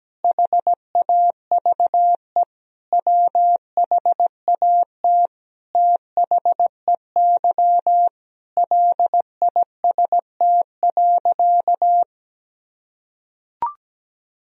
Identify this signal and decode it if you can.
Morse code, heard as a keyed tone, about 17 wpm, 700 Hz. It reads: HAVE WHAT THEY LIST.
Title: Sentences from Top 1000 Words - Review Part 2 - 17wpm